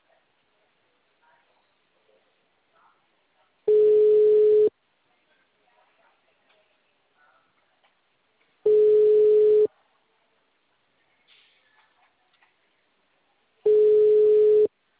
Ringback tone (Europe, 425 Hz).
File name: out- -2015- - - .41310